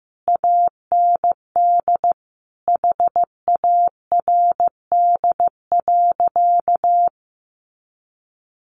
Morse code, heard as a keyed tone, about 15 wpm, 700 Hz. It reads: AND HARD.
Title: Sentences from Top 1300 Words 15wpm